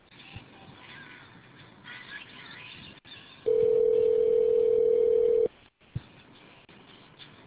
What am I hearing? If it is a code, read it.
Ringback tone (North America), 440 + 480 Hz.